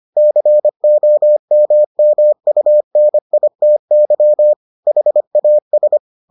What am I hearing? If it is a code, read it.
Morse code, heard as a keyed tone, about 25 wpm, 600 Hz. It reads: COMMUNITY HAS